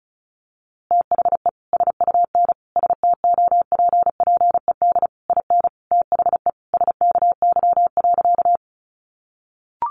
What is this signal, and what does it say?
Morse code, heard as a keyed tone, about 34 wpm, 700 Hz. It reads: THE SUN STOPPED IN THE SKY.
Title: A Sound Of Thunder 35wpm